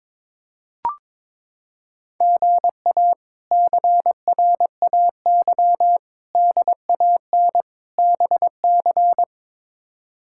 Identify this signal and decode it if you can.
Morse code, heard as a keyed tone, about 22 wpm, 700 Hz. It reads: GA CRAY DAN BC
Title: K1USN SST QSO Practice - 22wpm